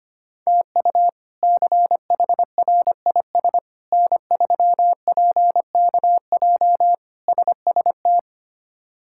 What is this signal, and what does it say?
Morse code, heard as a keyed tone, about 25 wpm, 700 Hz. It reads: TU CHRIS N3PKJ SST
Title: K1USN SST QSO Practice - 25wpm